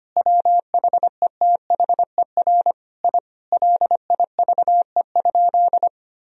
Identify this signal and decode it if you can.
Morse code, heard as a keyed tone, about 25 wpm, 700 Hz. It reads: WHETHER I LIVE?
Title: Sentences from Top 1000 Words - Review Part 4 - 25wpm